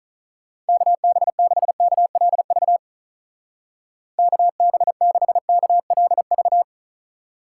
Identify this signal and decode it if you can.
Morse code, heard as a keyed tone, about 45 wpm, 700 Hz. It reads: KB6KLV KB6KLV